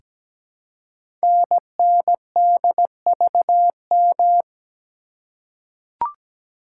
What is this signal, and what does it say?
Morse code, heard as a keyed tone, about 17 wpm, 700 Hz. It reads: NNDVM
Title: Code Groups - 5 Letters - 17wpm